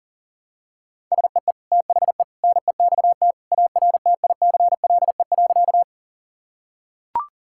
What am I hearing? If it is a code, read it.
Morse code, heard as a keyed tone, about 37 wpm, 700 Hz. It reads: SEE THE NEXT ARTICLE.
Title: Sentences from Top 1000 Words - Review Part 5 - 40wpm